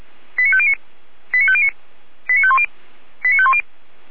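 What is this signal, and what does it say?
Five-tone selective call (ZVEI), heard as quick runs of five tones, tones of about 70 ms. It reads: 89509 89509 89510 89510
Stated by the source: ZVEI five-tone selective call received via SDR, from the Feuerwehr Orlamünde audio server.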